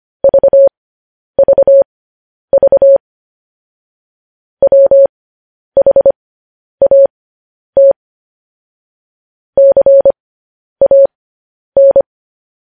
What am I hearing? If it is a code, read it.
Morse code, heard as a keyed tone, about 25 wpm, spaced as for 10 wpm, 550 Hz. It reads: VVV WHAT CAN
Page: news headlines in morse code at 10 wpm